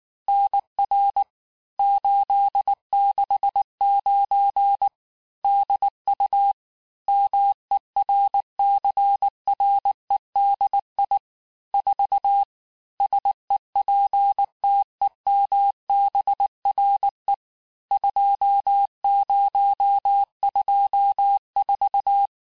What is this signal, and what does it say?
Morse code, heard as a keyed tone, about 19 wpm, 800 Hz. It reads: NR 869 DU MERCREDI 4 SEPTEMBRE 2024
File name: cours-du-mercredi / audios / cours-869-texte-CHIFFRE-vitesse-18-MPM